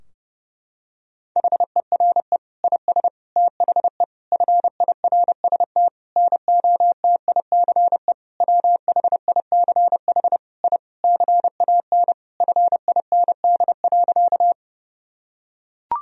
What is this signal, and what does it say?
Morse code, heard as a keyed tone, about 30 wpm, 700 Hz. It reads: HERE IS THE FIRST NOTICE WHICH I CAN FIND.
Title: Sentences from Top 1000 Words - Review Part 2 - 30wpm